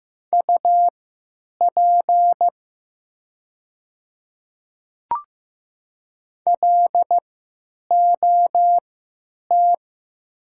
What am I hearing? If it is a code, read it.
Morse code, heard as a keyed tone, about 15 wpm, spaced as for 9 wpm, 700 Hz. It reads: UP LOT
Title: 3x Character Spacing - Top 500 Words - 15wpm